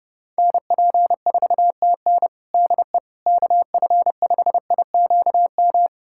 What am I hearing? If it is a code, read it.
Morse code, heard as a keyed tone, about 32 wpm, 700 Hz. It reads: NP4TN DE KF5IQM
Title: USA Callsign DE Callsign 30wpm